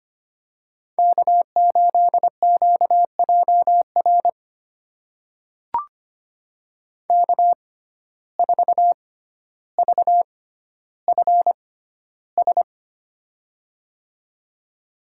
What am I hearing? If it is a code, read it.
Morse code, heard as a keyed tone, about 25 wpm, spaced as for 13 wpm, 700 Hz. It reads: K8QJR K4VFS
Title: USA Call Signs - 5 Characters - 6x Character Spacing - 25wpm